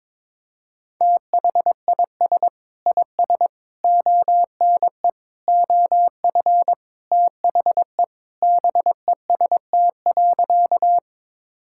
Morse code, heard as a keyed tone, about 22 wpm, 700 Hz. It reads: THIS IS ONE OF THE BEST.
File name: practice / 22wpm / Sentences with 4-Word Phrases - 22wpm